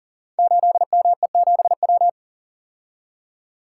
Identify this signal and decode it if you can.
Morse code, heard as a keyed tone, about 38 wpm, 690 Hz. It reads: 8ME7W
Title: Code Groups - 5 Letter:Numbers - 40wpm